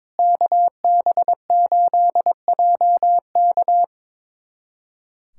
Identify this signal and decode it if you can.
Morse code, heard as a keyed tone, about 22 wpm, 700 Hz. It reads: KB8JK
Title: USA Callsign DE Callsign - 5 Characters - 22wpm